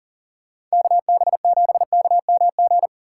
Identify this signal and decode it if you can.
Morse code, heard as a keyed tone, about 38 wpm, 700 Hz. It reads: KB7KMG
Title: USA Call Signs - 6 Characters - 4x Character Spacing - 40wpm